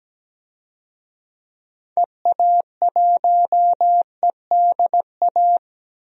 Morse code, heard as a keyed tone, about 17 wpm, 700 Hz. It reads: EA1EDA
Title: DXCC-100 Call Signs - 6 Characters - 17wpm